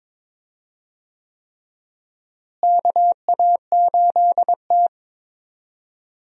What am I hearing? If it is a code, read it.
Morse code, heard as a keyed tone, about 22 wpm, 700 Hz. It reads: KA8T
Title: USA Call Signs - 4 Characters - Repeated 3x - 22wpm